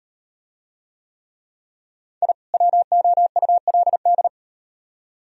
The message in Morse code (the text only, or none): I WOULD